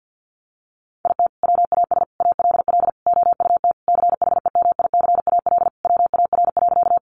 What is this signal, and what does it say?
Morse code, heard as a keyed tone, about 50 wpm, 700 Hz. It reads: IT WAS ALL OUT CHEMICAL WAR.
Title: Sentences from Top 1200 Words 50wpm